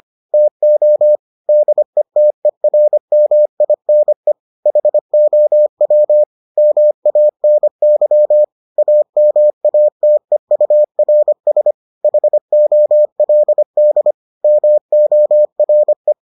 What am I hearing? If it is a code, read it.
Morse code, heard as a keyed tone, about 25 wpm, 600 Hz. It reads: TO DETERMINE HOW MANY AMATEURS HOLD MORE